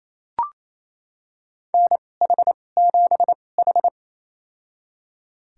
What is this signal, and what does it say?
Morse code, heard as a keyed tone, about 28 wpm, spaced as for 20 wpm, 700 Hz. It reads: NH7H
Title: USA Call Signs - 2x Character Spacing - 28wpm